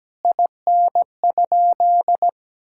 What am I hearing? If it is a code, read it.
Morse code, heard as a keyed tone, about 17 wpm, 700 Hz. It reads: IN?
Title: Sentences from Top 1600 Words 17wpm